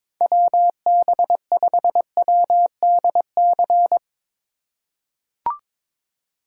Morse code, heard as a keyed tone, about 22 wpm, 700 Hz. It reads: WB5WDC
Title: USA Call Signs - 6 Characters - 3x Character Spacing - 22wpm